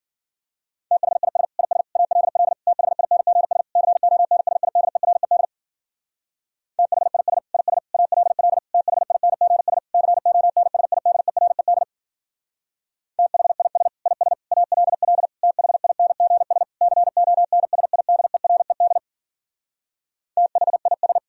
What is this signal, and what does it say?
Morse code, heard as a keyed tone, about 52 wpm, 700 Hz. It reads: THIS IS ALL THINGS CONSIDERED THIS IS ALL THINGS CONSIDERED THIS IS ALL THINGS CONSIDERED THIS